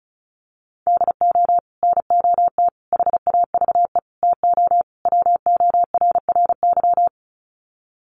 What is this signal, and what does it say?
Morse code, heard as a keyed tone, about 34 wpm, 700 Hz. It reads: DO NOT HAVE TO WORRY
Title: Sets of 5 Words 35wpm